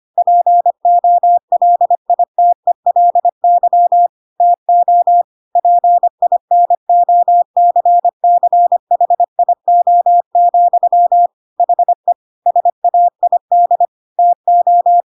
Morse code, heard as a keyed tone, about 25 wpm, 700 Hz. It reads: POLITELY TO PINOCCHIO, HE SAID TO
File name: Pinocchio / Pinocchio ch5 25 25 ik1hge